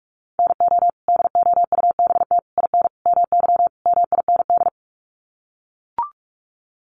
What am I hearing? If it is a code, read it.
Morse code, heard as a keyed tone, about 44 wpm, 700 Hz. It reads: NO DOUBT IN MY MIND
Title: Sets of 5 Words Encore2 45wpm